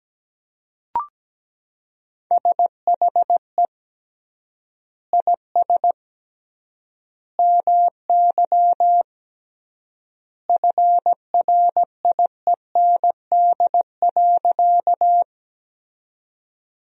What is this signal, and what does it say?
Morse code, heard as a keyed tone, about 17 wpm, 700 Hz. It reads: SHE IS MY FRIEND.
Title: 3x Word-Spacing - Sentences from Top 300 Words - 17wpm